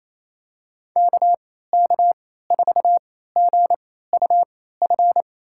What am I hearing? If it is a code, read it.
Morse code, heard as a keyed tone, about 28 wpm, spaced as for 16 wpm, 700 Hz. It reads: KK4GUF
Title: USA Call Signs - 6 Characters - 3x Character Spacing - 28wpm